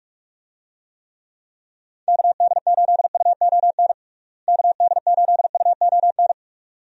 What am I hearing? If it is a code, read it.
Morse code, heard as a keyed tone, about 44 wpm, 700 Hz. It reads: KD8UON KD8UON
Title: USA Call Signs - 6 Characters - Repeated 2x - 45wpm